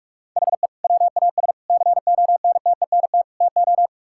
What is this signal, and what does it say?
Morse code, heard as a keyed tone, about 42 wpm, 690 Hz. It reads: HE WAS CONTENT TO
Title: Sentences from Top 1000 Words - Review Part 6 - 45wpm